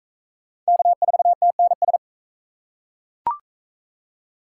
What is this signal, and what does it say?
Morse code, heard as a keyed tone, about 36 wpm, 700 Hz. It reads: K4TNS